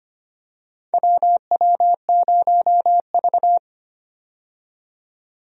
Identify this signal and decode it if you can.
Morse code, heard as a keyed tone, about 25 wpm, 700 Hz. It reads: WW0V
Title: USA Call Signs - 4 Characters - 25wpm